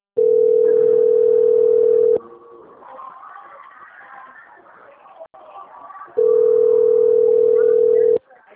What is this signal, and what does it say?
Ringback tone (North America), 440 + 480 Hz.